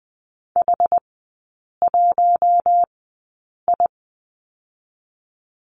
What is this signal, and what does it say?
Morse code, heard as a keyed tone, about 20 wpm, spaced as for 8 wpm, 700 Hz. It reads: H1I
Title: 178 Lesson 29e - Slash - Three Characters 20wpm